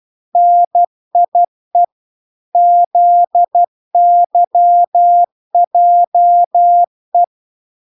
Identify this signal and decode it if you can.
Morse code, heard as a keyed tone, about 12 wpm, 700 Hz. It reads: NIE ZYJE